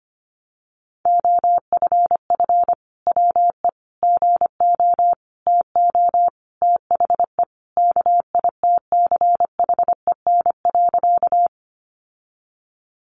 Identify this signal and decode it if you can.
Morse code, heard as a keyed tone, about 25 wpm, 700 Hz. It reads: OFF WE GO TO THE KITCHEN.
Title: Sentences from Top 1400 Words 25wpm